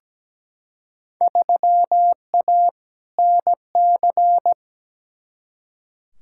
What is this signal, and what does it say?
Morse code, heard as a keyed tone, about 17 wpm, 700 Hz. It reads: 3A NC
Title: Field Day Exchanges - 17wpm